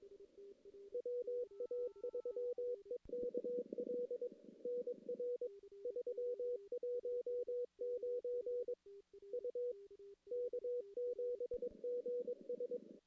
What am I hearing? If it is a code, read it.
Morse code, heard as a keyed tone, about 22 wpm, 490 Hz. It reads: WA3EKL NR 319 U K7GS
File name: CW / CW-SS-k5tr- -1700